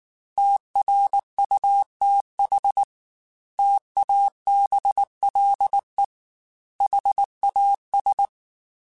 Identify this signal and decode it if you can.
Morse code, heard as a keyed tone, about 19 wpm, 800 Hz. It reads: TRUTH TABLE HAS
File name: Technical Subjects / Technical Subjects-mp3s / AND Gate @18